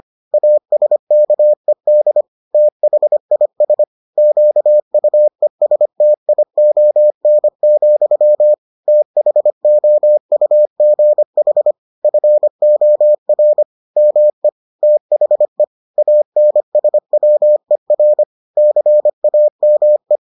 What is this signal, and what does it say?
Morse code, heard as a keyed tone, about 25 wpm, 600 Hz. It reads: ASKED THIS QUESTION, THOUGH FOR ME THE ANSWER CAME